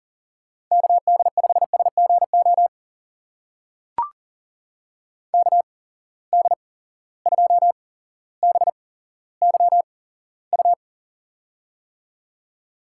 Morse code, heard as a keyed tone, about 40 wpm, spaced as for 17 wpm, 700 Hz. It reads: KD5SGO KD2BYU
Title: USA Call Signs - 6 Characters - 8x Character Spacing - 40wpm